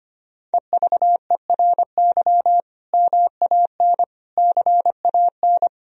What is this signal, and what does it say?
Morse code, heard as a keyed tone, about 25 wpm, 700 Hz. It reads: EVERY MAN CAN
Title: Sentences from Top 1000 Words - Review Part 1 - 25wpm